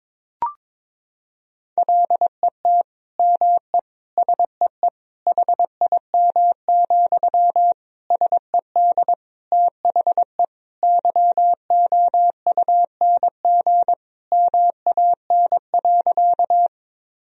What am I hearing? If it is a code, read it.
Morse code, heard as a keyed tone, about 22 wpm, 700 Hz. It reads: LET ME SEE HIM, SED THE YOUNG MAN.
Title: Sentences from Top 400 Words and Abbreviations 22wpm